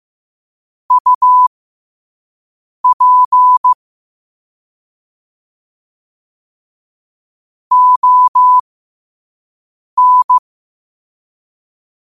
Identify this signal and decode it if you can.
Morse code, heard as a keyed tone, about 15 wpm, spaced as for 5 wpm, 1000 Hz. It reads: UP ON